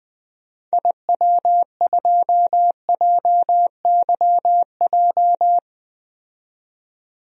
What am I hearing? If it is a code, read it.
Morse code, heard as a keyed tone, about 20 wpm, 700 Hz. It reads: IW2JYJ
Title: DXCC-100 Call Signs - 6 Characters - 20wpm